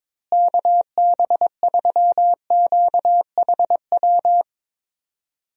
Morse code, heard as a keyed tone, about 22 wpm, 700 Hz. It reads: KB3QHW
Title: USA Call Signs - 6 Characters - Repeated 2x - 22wpm